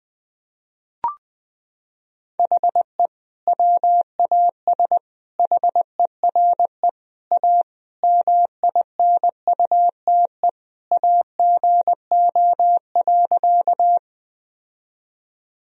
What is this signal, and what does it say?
Morse code, heard as a keyed tone, about 21 wpm, 700 Hz. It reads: HE WAS HERE A MINUTE AGO.
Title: Sentences from Top 1000 Words - Review Part 3 - 20wpm